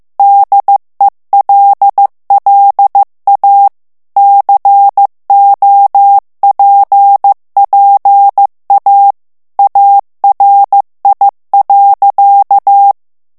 Morse code, heard as a keyed tone, about 15 wpm, 800 Hz. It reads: DELLA COPPA ARI.
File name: morse mp3 / ITALIEN texte clair 2 vitesse 14 mots minute